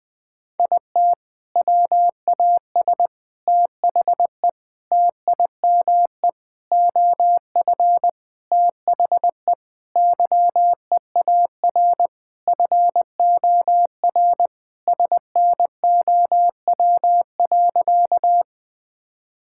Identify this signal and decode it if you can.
Morse code, heard as a keyed tone, about 21 wpm, 700 Hz. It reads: IT WAS THE TIME OF THE YEAR FOR SNOW.
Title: Sentences with 4-Word Phrases - Spoken Beginning - 20wpm